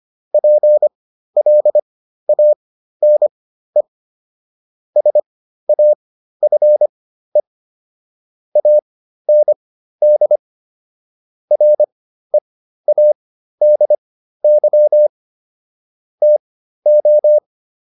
Morse code, heard as a keyed tone, about 25 wpm, spaced as for 13 wpm, 600 Hz. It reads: PLANE SAFE AND READY TO